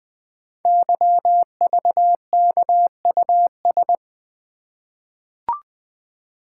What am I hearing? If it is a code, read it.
Morse code, heard as a keyed tone, about 20 wpm, 700 Hz. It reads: YVKUS
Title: Code Groups - 5 Letters - 20wpm